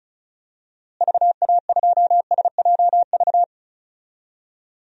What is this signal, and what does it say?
Morse code, heard as a keyed tone, about 36 wpm, 700 Hz. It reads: VA2SJV
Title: DXCC-100 Call Signs - 6 Characters - Repeated 2x - 35wpm